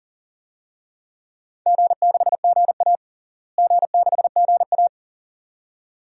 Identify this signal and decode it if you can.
Morse code, heard as a keyed tone, about 37 wpm, 700 Hz. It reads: G6GA G6GA
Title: DXCC-100 Call Signs - 4 Characters - Repeated 2x - 40wpm